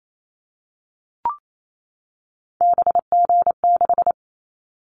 Morse code, heard as a keyed tone, about 28 wpm, 700 Hz. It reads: BG6
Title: Callsign Prefixes for Top 100 DXCC - 28wpm